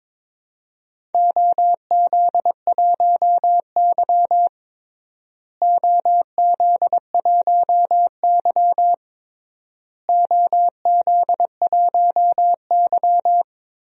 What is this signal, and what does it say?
Morse code, heard as a keyed tone, about 22 wpm, 700 Hz. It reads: OZ1Y OZ1Y OZ1Y
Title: DXCC-100 Call Signs - 4 Characters - Repeated 3x - 22wpm